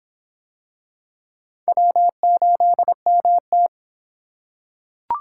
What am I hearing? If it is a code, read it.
Morse code, heard as a keyed tone, about 24 wpm, 700 Hz. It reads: W8MT